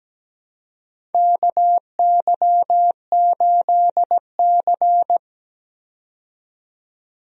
Morse code, heard as a keyed tone, about 17 wpm, 700 Hz. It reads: KY8C
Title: USA Call Signs - 4 Characters - 17wpm